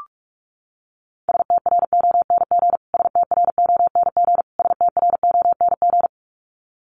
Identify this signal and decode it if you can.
Morse code, heard as a keyed tone, about 44 wpm, 700 Hz. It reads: STRONG STRONG STRONG
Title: Top 500 Words Repeated 45wpm